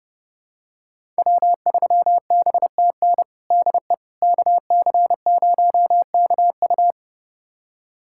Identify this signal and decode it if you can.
Morse code, heard as a keyed tone, about 30 wpm, 700 Hz. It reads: W3BTN DE KC0KU
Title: USA Callsign DE Callsign - 5 Characters - 30wpm